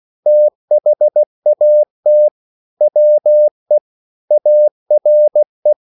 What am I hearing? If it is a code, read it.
Morse code, heard as a keyed tone, about 16 wpm, 600 Hz. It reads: THAT WE ARE
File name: CW / Practice / EnglishQuotations0416